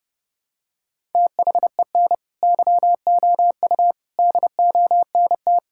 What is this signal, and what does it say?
Morse code, heard as a keyed tone, about 30 wpm, 700 Hz. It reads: THEN YOU DONT
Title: Sentences from Top 1000 Words - Review Part 6 - 30wpm